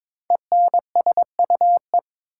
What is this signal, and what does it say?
Morse code, heard as a keyed tone, about 22 wpm, 690 Hz. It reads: ENSUE
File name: 5 Letter Words - First 3 Letters Given - 22wpm